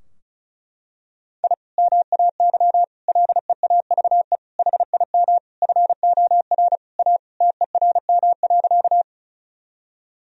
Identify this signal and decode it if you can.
Morse code, heard as a keyed tone, about 36 wpm, 700 Hz. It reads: I MAY LEAVE HIM FOR A TERM.
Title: Sentences from Top 1000 Words - Review Part 4 - 35wpm